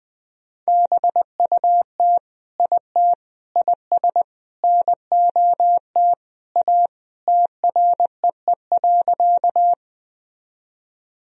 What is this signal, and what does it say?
Morse code, heard as a keyed tone, about 20 wpm, 700 Hz. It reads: BUT IT IS NOT A TREE.
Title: Sentences from Top 1000 Words - Review Part 1 - 20wpm